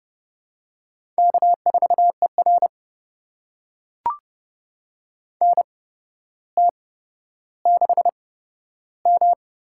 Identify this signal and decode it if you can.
Morse code, heard as a keyed tone, about 30 wpm, spaced as for 13 wpm, 700 Hz. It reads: K4ER NT6M
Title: USA Call Signs - 4 Characters - 8x Character Spacing - 30wpm